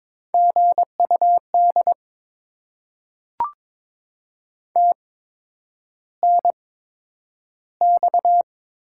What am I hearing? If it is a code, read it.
Morse code, heard as a keyed tone, about 22 wpm, spaced as for 9 wpm, 700 Hz. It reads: GUD TNX
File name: Top 20 QSO Elements - 8x Character Spacing - 22wpm